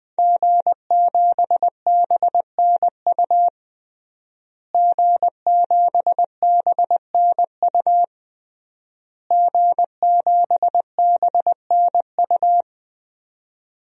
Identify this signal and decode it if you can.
Morse code, heard as a keyed tone, about 20 wpm, 700 Hz. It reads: G7BNU G7BNU G7BNU
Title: DXCC-100 Call Signs - 5 Characters - Repeated 3x - 20wpm